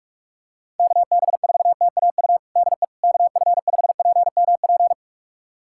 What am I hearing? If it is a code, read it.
Morse code, heard as a keyed tone, about 46 wpm, 700 Hz. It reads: KB4TAU DE KF5PMP
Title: USA Callsign DE Callsign - 6 Characters - 45wpm